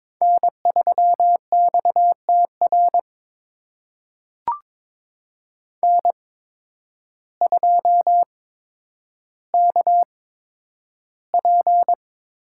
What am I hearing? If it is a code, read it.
Morse code, heard as a keyed tone, about 22 wpm, spaced as for 10 wpm, 700 Hz. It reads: N3XTR N2KP
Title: USA Call Signs - 5 Characters - 8x Character Spacing - 22wpm